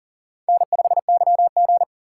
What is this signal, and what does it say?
Morse code, heard as a keyed tone, about 40 wpm, 700 Hz. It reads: N5YG
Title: USA Call Signs - 4 Characters - 2x Character Spacing - 40wpm